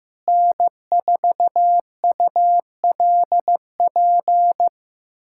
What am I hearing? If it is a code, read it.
Morse code, heard as a keyed tone, about 16 wpm, 700 Hz. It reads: N4ULP
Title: USA Callsign DE Callsign - 5 Characters - 15wpm